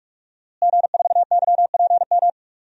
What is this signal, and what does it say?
Morse code, heard as a keyed tone, about 43 wpm, 700 Hz. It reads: G4YPM
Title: DXCC-100 Call Signs - 5 Characters - Repeated 2x - 45wpm